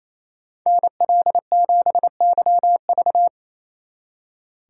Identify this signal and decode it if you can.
Morse code, heard as a keyed tone, about 27 wpm, 700 Hz. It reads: NL7YV
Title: Speed Racing - USA 2x2 Call Signs 28wpm